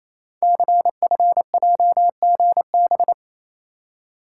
Morse code, heard as a keyed tone, about 29 wpm, 700 Hz. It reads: CFJGB
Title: Code Groups - 5 Letters - 28wpm